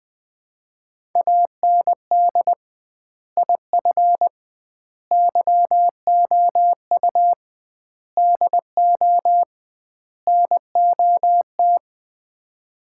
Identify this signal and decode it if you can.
Morse code, heard as a keyed tone, about 20 wpm, 700 Hz. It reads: AND IF YOU DO NOT
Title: 2x Word-Spacing - Sets of 5 Words - 20wpm